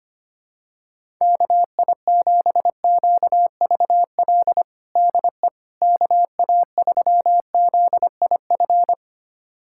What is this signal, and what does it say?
Morse code, heard as a keyed tone, about 25 wpm, 700 Hz. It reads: KI7QVL DE KA3ZIF
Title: USA Callsign DE Callsign 25wpm